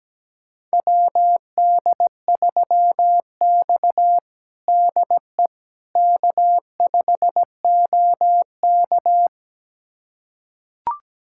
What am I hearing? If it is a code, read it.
Morse code, heard as a keyed tone, about 17 wpm, 700 Hz. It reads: WD3X DE K5OK
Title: USA Callsign DE Callsign - 4 Characters - 17wpm